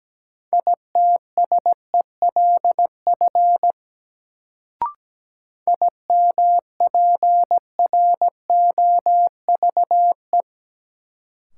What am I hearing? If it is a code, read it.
Morse code, heard as a keyed tone, about 17 wpm, 700 Hz. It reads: ITSELF IMPROVE
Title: Sentences from Top 800 Words - New Word Drill - 17wpm